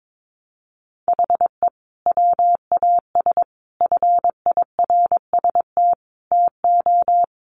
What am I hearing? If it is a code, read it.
Morse code, heard as a keyed tone, about 22 wpm, 700 Hz. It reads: HE WAS FIRST TO
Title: Sentences from Top 1100 Words 22wpm